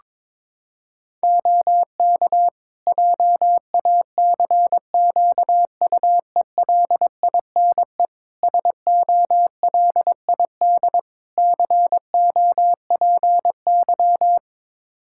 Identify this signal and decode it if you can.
Morse code, heard as a keyed tone, about 22 wpm, 700 Hz. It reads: OK JACQUELINE SOLID COPY
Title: K1USN SST QSO Practice 22wpm